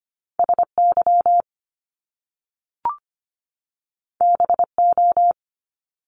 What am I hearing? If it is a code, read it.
Morse code, heard as a keyed tone, about 25 wpm, 700 Hz. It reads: SY BO
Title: Top Two Letter Combinations in Words 25wpm